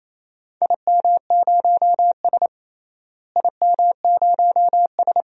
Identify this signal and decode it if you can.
Morse code, heard as a keyed tone, about 28 wpm, 700 Hz. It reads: IM0S IM0S
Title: DXCC-100 Call Signs - 4 Characters - Repeated 2x - 28wpm